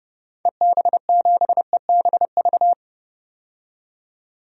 Morse code, heard as a keyed tone, about 30 wpm, 690 Hz. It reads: EB7EBV
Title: DXCC-100 Call Signs - 6 Characters - 30wpm